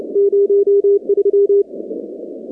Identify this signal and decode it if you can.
Morse code, heard as a keyed tone, about 30 wpm, 400 Hz. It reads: 03
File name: contests / 2019-cw-wpx-kt5j-k5pi- -2330 0-46